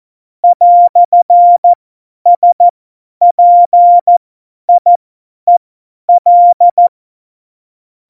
Morse code, heard as a keyed tone, about 14 wpm, spaced as for 10 wpm, 700 Hz. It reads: "SPIEL
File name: Morsekurs / Witze / Witz 21. August 2018, 50 F70